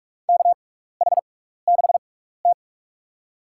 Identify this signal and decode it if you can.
Morse code, heard as a keyed tone, about 47 wpm, spaced as for 16 wpm, 700 Hz. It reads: KH6T
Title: USA Call Signs - 4 Characters - 6x Character Spacing - 45wpm